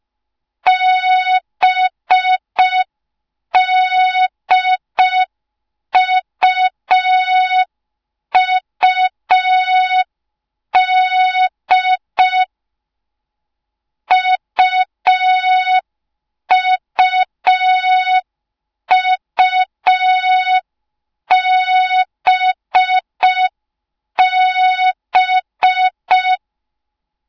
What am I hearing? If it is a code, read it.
Morse code, heard as a keyed tone, about 5 wpm, 3700 Hz. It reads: BDUUD UUUBB